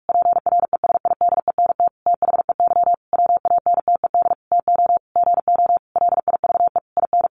Morse code, heard as a keyed tone, about 44 wpm, 700 Hz. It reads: PRESIDENT THEY WANTED TO GO LIVE IN
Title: A Sound Of Thunder 45wpm